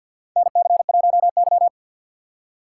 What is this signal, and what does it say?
Morse code, heard as a keyed tone, about 53 wpm, 700 Hz. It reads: NC1Y
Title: USA Call Signs - 4 Characters - 6x Character Spacing - 50wpm